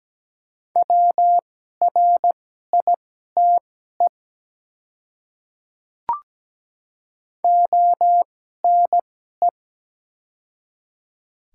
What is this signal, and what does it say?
Morse code, heard as a keyed tone, about 17 wpm, spaced as for 12 wpm, 700 Hz. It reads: WRITE ONE
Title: 2x Character Spacing - Top 100 Words - 17wpm